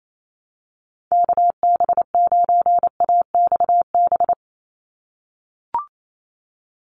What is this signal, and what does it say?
Morse code, heard as a keyed tone, about 28 wpm, 700 Hz. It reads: KB9AXB
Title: USA Call Signs 28wpm